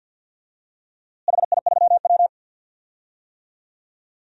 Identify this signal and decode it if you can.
Morse code, heard as a keyed tone, about 51 wpm, 700 Hz. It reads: HI3W
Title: DXCC-100 Call Signs - 4 Characters - Repeated 2x - 50wpm